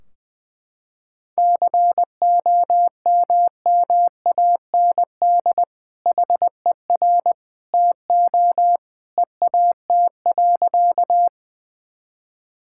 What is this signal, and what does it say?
Morse code, heard as a keyed tone, about 21 wpm, 700 Hz. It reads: COMMAND HER TO EAT.
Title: Sentences from Top 1600 Words 20wpm